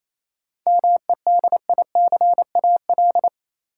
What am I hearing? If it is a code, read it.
Morse code, heard as a keyed tone, about 28 wpm, 700 Hz. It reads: MEDICAL
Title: Sentences from Top 800 Words - New Word Drill - 28wpm